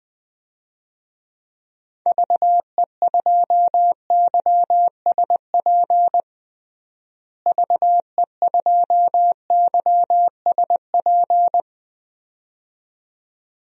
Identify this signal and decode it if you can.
Morse code, heard as a keyed tone, about 20 wpm, 700 Hz. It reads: VE2YSP VE2YSP
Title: DXCC-100 Call Signs - 6 Characters - Repeated 2x - 20wpm